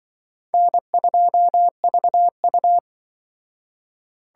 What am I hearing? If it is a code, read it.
Morse code, heard as a keyed tone, about 24 wpm, 700 Hz. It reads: N2VU
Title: Speed Racing - USA Call Signs 20wpm